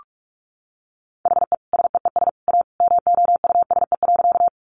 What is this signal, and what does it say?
Morse code, heard as a keyed tone, about 44 wpm, 700 Hz. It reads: HE SEES A MOUSE.
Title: Easy Phrases 45wpm